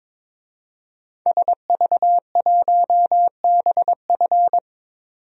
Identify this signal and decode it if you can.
Morse code, heard as a keyed tone, about 22 wpm, 700 Hz. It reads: SV1BF
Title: DXCC-100 Call Signs - 5 Characters - 22wpm